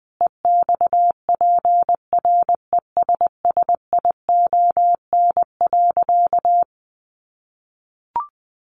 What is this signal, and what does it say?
Morse code, heard as a keyed tone, about 20 wpm, 700 Hz. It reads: EXPRESSION.
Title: Sentences from Top 1200 Words 20wpm